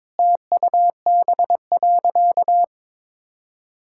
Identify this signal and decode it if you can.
Morse code, heard as a keyed tone, about 22 wpm, 700 Hz. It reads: TUB.
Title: Pangram Challenge v2 - 22wpm